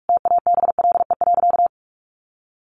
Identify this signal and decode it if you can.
Morse code, heard as a keyed tone, about 44 wpm, 700 Hz. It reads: TABLE.
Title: Sentences from Top 1400 Words 45wpm